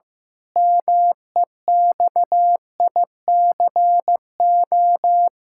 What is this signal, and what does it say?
Morse code, heard as a keyed tone, about 15 wpm, 700 Hz. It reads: MEXICO
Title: Pangram Challenge v2 - 15wpm